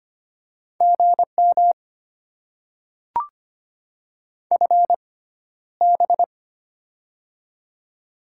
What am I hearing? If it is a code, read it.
Morse code, heard as a keyed tone, about 25 wpm, spaced as for 13 wpm, 690 Hz. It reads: GM FB
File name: Top 20 QSO Elements - 6x Character Spacing - 25wpm